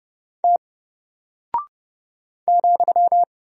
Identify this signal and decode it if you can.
Morse code, heard as a keyed tone, about 30 wpm, 700 Hz. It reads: T ,
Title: Single Character (Letter:Number:Punctuation:Prosign) 30wpm